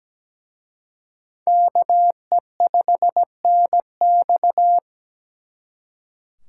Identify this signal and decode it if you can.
Morse code, heard as a keyed tone, about 21 wpm, 700 Hz. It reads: KE5NX
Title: Speed Racing - USA 2x2 Call Signs 17wpm